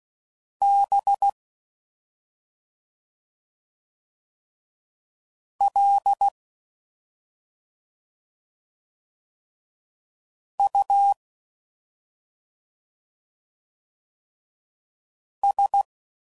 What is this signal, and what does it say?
Morse code, heard as a keyed tone, about 16 wpm, 800 Hz. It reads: B L U S
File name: Suffixes / Special ING's / Sngl ING's